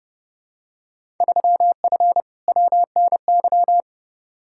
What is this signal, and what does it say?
Morse code, heard as a keyed tone, about 30 wpm, 700 Hz. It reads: 3F WNY
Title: Field Day Exchanges - 30wpm